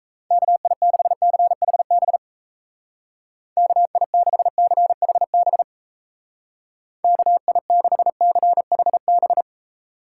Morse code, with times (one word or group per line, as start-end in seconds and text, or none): 0.3-2.2 KI6CHB
3.6-5.6 KI6CHB
7.0-9.4 KI6CHB